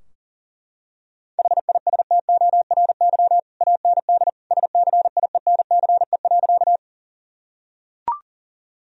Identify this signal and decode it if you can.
Morse code, heard as a keyed tone, about 40 wpm, 700 Hz. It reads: HISTORY AND SCIENCE.
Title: Sentences from Top 1000 Words - Review Part 2 - 40wpm